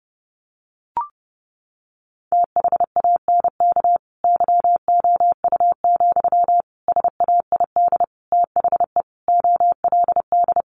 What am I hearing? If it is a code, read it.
Morse code, heard as a keyed tone, about 30 wpm, 700 Hz. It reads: THANK YOU, SAID THE OLD
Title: Sentences from Top 1100 Words 30wpm